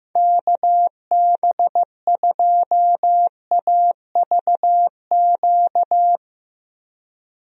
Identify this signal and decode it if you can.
Morse code, heard as a keyed tone, about 15 wpm, 700 Hz. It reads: KB2AVQ
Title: USA Call Signs - 6 Characters - 15wpm